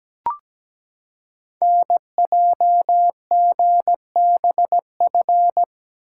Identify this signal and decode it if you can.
Morse code, heard as a keyed tone, about 17 wpm, 700 Hz. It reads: NJGBF
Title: Code Groups - 5 Letters - 17wpm